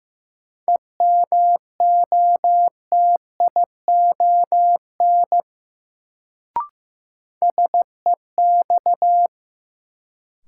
Morse code, heard as a keyed tone, about 15 wpm, 700 Hz. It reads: EMOTION SEX
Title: Sentences from Top 1600 Words - New Word Drill - 15wpm